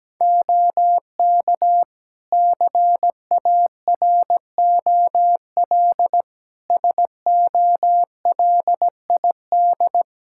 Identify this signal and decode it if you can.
Morse code, heard as a keyed tone, about 17 wpm, 700 Hz. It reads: OK CAROL SOLID
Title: K1USN SST QSO Practice 17wpm